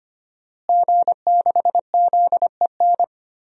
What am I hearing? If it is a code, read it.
Morse code, heard as a keyed tone, about 25 wpm, 700 Hz. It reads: G6ZEN
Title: DXCC-100 Call Signs - 5 Characters - Repeated 3x - 25wpm